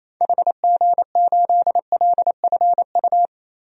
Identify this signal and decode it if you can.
Morse code, heard as a keyed tone, about 27 wpm, 700 Hz. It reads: HG8LFU
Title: DXCC-100 Call Signs - 6 Characters - 28wpm